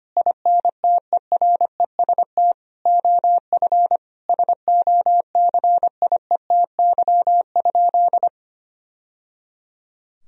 Morse code, heard as a keyed tone, about 25 wpm, 700 Hz. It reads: INTEREST OF SOCIETY?
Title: Sentences from Top 1000 Words - Review Part 4 - 25wpm